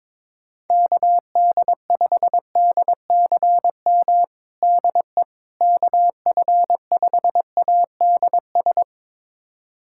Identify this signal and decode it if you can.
Morse code, heard as a keyed tone, about 23 wpm, 700 Hz. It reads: KD5DCM DE KF5ADS
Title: USA Callsign DE Callsign 22wpm